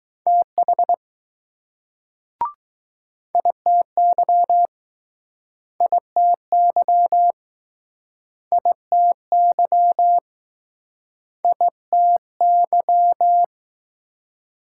Morse code, heard as a keyed tone, about 20 wpm, 700 Hz. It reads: TH ITY ITY ITY ITY